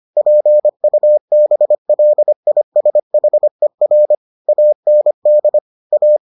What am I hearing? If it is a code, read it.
Morse code, heard as a keyed tone, about 25 wpm, 600 Hz. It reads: PUBLISHER AND A